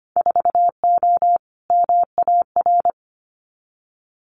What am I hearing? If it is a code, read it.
Morse code, heard as a keyed tone, about 25 wpm, 700 Hz. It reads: 4O MAR
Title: Winter Field Day Exchanges Encore 25wpm